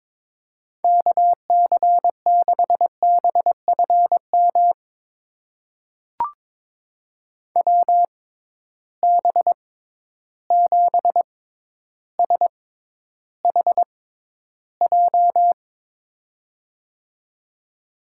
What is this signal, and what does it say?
Morse code, heard as a keyed tone, about 22 wpm, spaced as for 11 wpm, 700 Hz. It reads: KC6BFM WB7SHJ
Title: USA Call Signs - 6 Characters - 6x Character Spacing - 22wpm